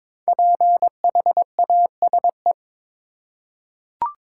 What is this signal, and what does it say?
Morse code, heard as a keyed tone, about 22 wpm, 700 Hz. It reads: PHASE
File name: 5 Letter Words - First 3 Letters Given - 22wpm